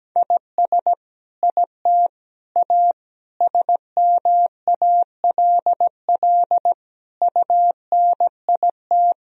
Morse code, heard as a keyed tone, about 17 wpm, 700 Hz. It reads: IS IT A SMALL UNIT
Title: Sentences from Top 1000 Words - Review Part 2 - 17wpm